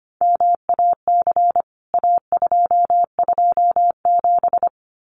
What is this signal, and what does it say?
Morse code, heard as a keyed tone, about 25 wpm, 700 Hz. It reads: MAC A227
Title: CWT Exchange 25wpm